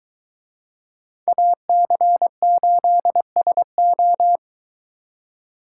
Morse code, heard as a keyed tone, about 23 wpm, 700 Hz. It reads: AC8SO